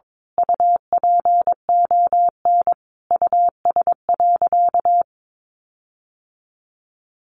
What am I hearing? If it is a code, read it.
Morse code, heard as a keyed tone, about 22 wpm, 700 Hz. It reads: UPON US.
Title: Sentences from Top 1500 Words 22wpm